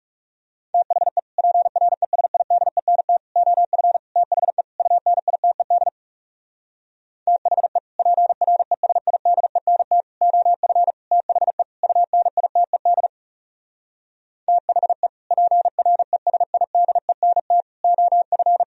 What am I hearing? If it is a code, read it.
Morse code, heard as a keyed tone, about 40 wpm, 700 Hz. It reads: THE PRESIDENT OF THE UNITED THE PRESIDENT OF THE UNITED THE PRESIDENT OF